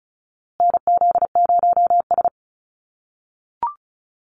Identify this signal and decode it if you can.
Morse code, heard as a keyed tone, about 34 wpm, 700 Hz. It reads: NZ0S
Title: USA Call Signs Repeated 3x 35wpm